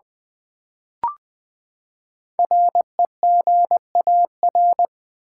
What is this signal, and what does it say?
Morse code, heard as a keyed tone, about 20 wpm, 700 Hz. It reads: REGAR
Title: Build A Word - 5 Letters At A Time - 10-Letter Words - 20wpm